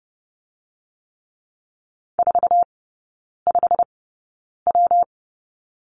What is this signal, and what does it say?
Morse code, heard as a keyed tone, about 30 wpm, 700 Hz. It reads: 4 5 W
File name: Lesson 19e - Letter w - Three Characters 30wpm